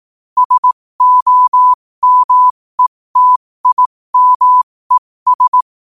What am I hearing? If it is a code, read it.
Morse code, heard as a keyed tone, about 18 wpm, spaced as for 15 wpm, 1000 Hz. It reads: SOMETIMES
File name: qotdfiles / 15wpm